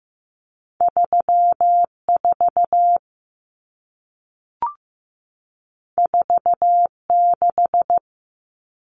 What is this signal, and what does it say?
Morse code, heard as a keyed tone, about 15 wpm, 700 Hz. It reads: 34 46
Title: Common Cut Numbers - 2-Digit 15wpm